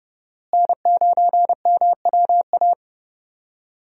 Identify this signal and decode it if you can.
Morse code, heard as a keyed tone, about 30 wpm, 700 Hz. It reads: N9MWA
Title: USA Call Signs - 5 Characters - Repeated 3x - 30wpm